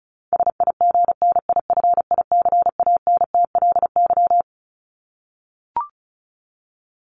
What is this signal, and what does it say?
Morse code, heard as a keyed tone, about 34 wpm, 700 Hz. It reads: SIGNIFICANTLY
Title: LY Suffix - Top 100 Words with Base Words 35wpm